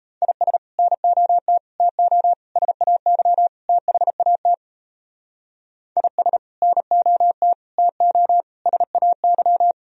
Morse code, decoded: IS NOT TO SAY THAT IS NOT TO SAY

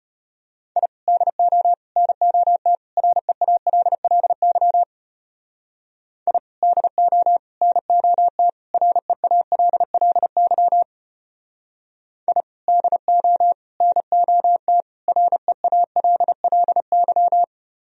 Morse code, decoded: I DO NOT REALLY I DO NOT REALLY I DO NOT REALLY

